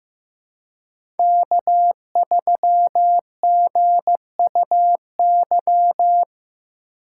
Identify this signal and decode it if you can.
Morse code, heard as a keyed tone, about 15 wpm, 700 Hz. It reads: K3GUY
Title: USA Call Signs - 2x Character Spacing - 15wpm